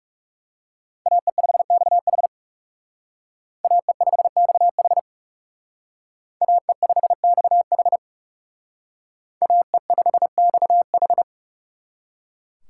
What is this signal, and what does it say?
Morse code, heard as a keyed tone, about 36 wpm, 700 Hz. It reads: AE5XH AE5XH AE5XH AE5XH
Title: Speed Racing - USA 2x2 Call Signs 30wpm